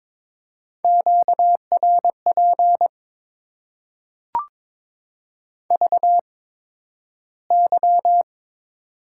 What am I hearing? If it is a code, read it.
Morse code, heard as a keyed tone, about 22 wpm, spaced as for 12 wpm, 700 Hz. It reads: QRP VY
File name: Top 90 QSO Elements - 8x Character Spacing - 22wpm